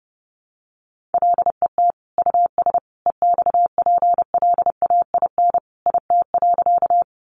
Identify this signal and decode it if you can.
Morse code, heard as a keyed tone, about 30 wpm, 700 Hz. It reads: LET US EXPLAIN IT.
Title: Sentences from Top 1100 Words 30wpm